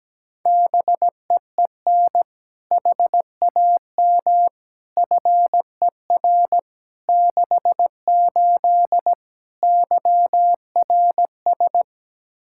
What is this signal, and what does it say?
Morse code, heard as a keyed tone, about 17 wpm, 700 Hz. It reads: BEEN HAM FER 68 YRS
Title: Practice QSO - Exchange 3 - More Info - 17wpm